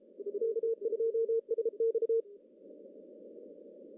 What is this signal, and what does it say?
Morse code, heard as a keyed tone, about 33 wpm, 460 Hz. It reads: K2SX
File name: CW / cqww-cw-k5tr